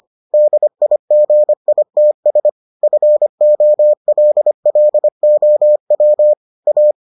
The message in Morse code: DIGITS FOLLOW A

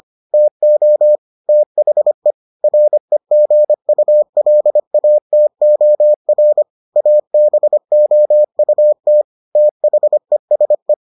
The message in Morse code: TO THE REGULATOR ABOUT THESE

